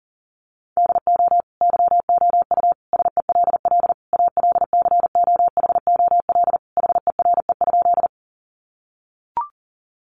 Morse code, heard as a keyed tone, about 40 wpm, 700 Hz. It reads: DO YOU SELL ALCOHOL HERE?
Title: Sentences from Top 1400 Words 40wpm